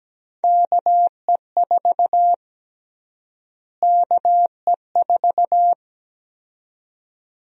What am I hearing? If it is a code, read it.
Morse code, heard as a keyed tone, about 17 wpm, 700 Hz. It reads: KE4 KE4
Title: USA Call Sign Prefixes - Repeated 2x - 17wpm